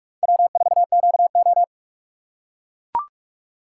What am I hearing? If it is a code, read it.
Morse code, heard as a keyed tone, about 41 wpm, 700 Hz. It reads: W4QO